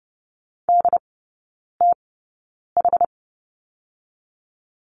Morse code, heard as a keyed tone, about 30 wpm, spaced as for 9 wpm, 690 Hz. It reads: DTH
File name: Lesson 16e - Letter c - Three Characters 30wpm